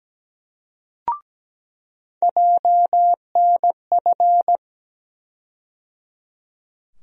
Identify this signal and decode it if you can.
Morse code, heard as a keyed tone, about 17 wpm, 700 Hz. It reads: JNF
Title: USA Call Sign Postfixes - 3 Letters 17wpm